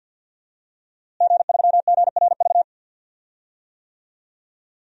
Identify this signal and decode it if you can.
Morse code, heard as a keyed tone, about 46 wpm, 700 Hz. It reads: G3GRV
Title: DXCC-100 Call Signs - 5 Characters - Repeated 3x - 50wpm